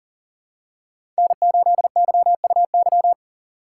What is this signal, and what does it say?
Morse code, heard as a keyed tone, about 40 wpm, 700 Hz. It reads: N8YUY
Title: USA Call Signs - 4x Character Spacing - 40wpm